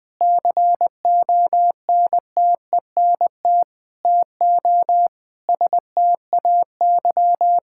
Morse code, heard as a keyed tone, about 21 wpm, 700 Hz. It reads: CONTENT TO STAY
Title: Sentences from Top 1000 Words - Review Part 6 - 20wpm